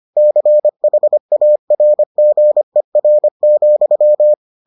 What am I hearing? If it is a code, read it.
Morse code, heard as a keyed tone, about 25 wpm, 600 Hz. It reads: CHARGER,